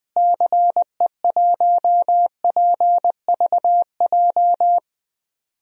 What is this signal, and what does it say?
Morse code, heard as a keyed tone, about 21 wpm, 700 Hz. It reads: CE1PVJ
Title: DXCC-100 Call Signs - 6 Characters - Repeated 3x - 20wpm